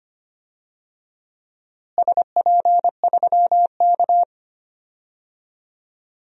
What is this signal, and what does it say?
Morse code, heard as a keyed tone, about 25 wpm, 700 Hz. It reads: SP3K
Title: DXCC-100 Call Signs - 4 Characters - Repeated 3x - 25wpm